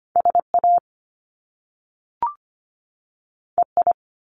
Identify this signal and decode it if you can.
Morse code, heard as a keyed tone, about 25 wpm, 700 Hz. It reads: SA EI
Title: Top Two Letter Combinations in Words 25wpm